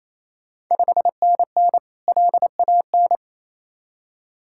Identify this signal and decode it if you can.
Morse code, heard as a keyed tone, about 28 wpm, 700 Hz. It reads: 5NN LAN
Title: Ontario QSO Party Exchange 28wpm